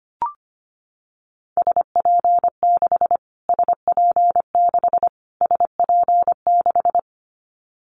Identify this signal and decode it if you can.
Morse code, heard as a keyed tone, about 25 wpm, 700 Hz. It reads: SP6 SP6 SP6
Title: Callsign Prefixes for Top 25 DXCC - Repeated 3x - 25wpm